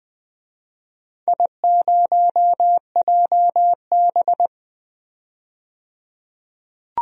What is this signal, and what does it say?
Morse code, heard as a keyed tone, about 20 wpm, 700 Hz. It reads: I0JB
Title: DXCC-100 Call Signs - 4 Characters - 20wpm